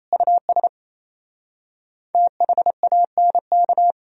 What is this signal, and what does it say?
Morse code, heard as a keyed tone, about 37 wpm, 700 Hz. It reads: US THANK